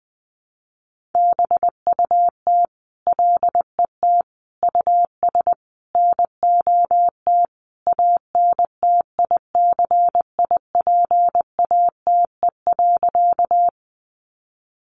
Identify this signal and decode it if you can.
Morse code, heard as a keyed tone, about 20 wpm, 700 Hz. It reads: BUT LET US NOT ANTICIPATE.
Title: Sentences from Top 1500 Words 20wpm